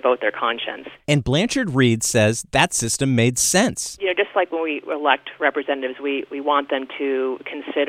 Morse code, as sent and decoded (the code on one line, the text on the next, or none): none